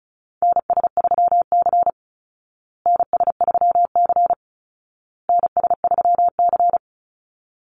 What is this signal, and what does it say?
Morse code, heard as a keyed tone, about 34 wpm, 700 Hz. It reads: NS3C NS3C NS3C
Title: Call Signs Repeated 2x - 35wpm